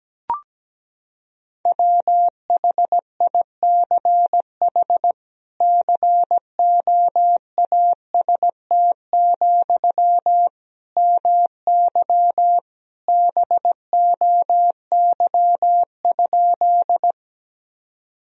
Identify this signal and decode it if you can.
Morse code, heard as a keyed tone, about 17 wpm, 700 Hz. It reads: WHICH COAST, MY BOY?
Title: Sentences from Top 1000 Words - Review Part 4 - 17wpm